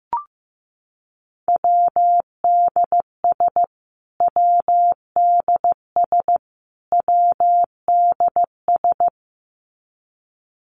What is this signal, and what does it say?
Morse code, heard as a keyed tone, about 15 wpm, 700 Hz. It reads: WDS WDS WDS
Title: Abbreviations Repeated 3x - 15wpm